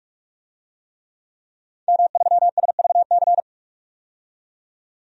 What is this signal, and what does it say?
Morse code, heard as a keyed tone, about 44 wpm, 700 Hz. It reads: M3SVC
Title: DXCC-100 Call Signs - 5 Characters - 45wpm